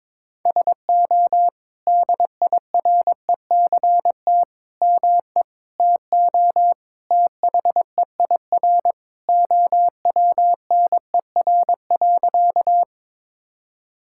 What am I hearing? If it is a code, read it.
Morse code, heard as a keyed tone, about 22 wpm, 700 Hz. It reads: SO DIRECT ME TO THEIR OWNER.
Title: Sentences from Top 1600 Words 22wpm